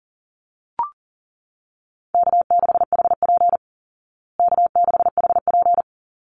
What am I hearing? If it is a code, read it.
Morse code, heard as a keyed tone, about 40 wpm, 700 Hz. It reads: K6HP K6HP
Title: Easy Call Signs Repeated 2x - 40wpm